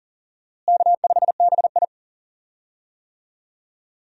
Morse code, heard as a keyed tone, about 40 wpm, 700 Hz. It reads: K5BI
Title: USA Call Signs - 4 Characters - Repeated 2x - 40wpm